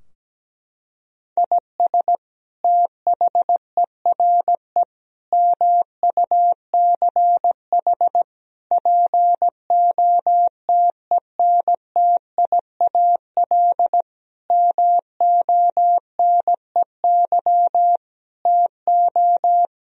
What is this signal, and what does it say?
Morse code, heard as a keyed tone, about 17 wpm, 700 Hz. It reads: IS THERE MUCH POTENTIAL MONEY TO
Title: Sentences from Top 1000 Words - Review Part 4 - 17wpm